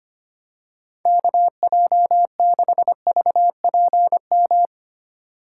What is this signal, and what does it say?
Morse code, heard as a keyed tone, about 25 wpm, 700 Hz. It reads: KJ6VPM
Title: USA Call Signs - 6 Characters - 4x Character Spacing - 25wpm